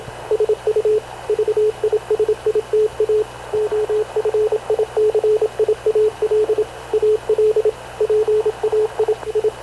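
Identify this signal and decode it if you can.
Morse code, heard as a keyed tone, about 27 wpm, 430 Hz, with radio noise. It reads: SU VISITA OFICIAL AL PAI I
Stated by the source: Havana Radio / CLA sending Spanish press (PX) in Morse, recorded by Radio Officer Paul Dumesnil onboard CNAV QUEST/CZDO.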